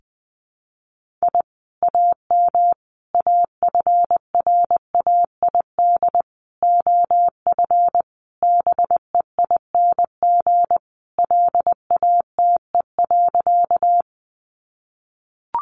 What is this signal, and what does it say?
Morse code, heard as a keyed tone, about 20 wpm, 700 Hz. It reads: I AM AFRAID OF BEING LATE.
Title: Sentences from Top 1500 Words 20wpm